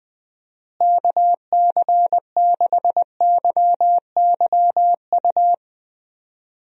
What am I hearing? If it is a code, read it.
Morse code, heard as a keyed tone, about 20 wpm, 700 Hz. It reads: KC6YYU
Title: USA Call Signs - 6 Characters - Repeated 2x - 20wpm